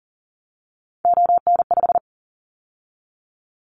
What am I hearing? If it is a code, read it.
Morse code, heard as a keyed tone, about 40 wpm, 700 Hz. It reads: ON5
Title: Callsign Prefixes for Top 50 DXCC - 40wpm